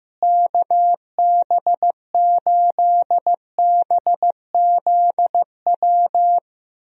Morse code, heard as a keyed tone, about 16 wpm, 700 Hz. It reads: KB8BZW
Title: USA Call Signs - 6 Characters - Repeated 3x - 15wpm